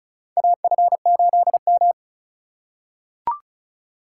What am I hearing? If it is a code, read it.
Morse code, heard as a keyed tone, about 36 wpm, 700 Hz. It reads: AF8M
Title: USA Call Signs - 4 Characters - 4x Character Spacing - 35wpm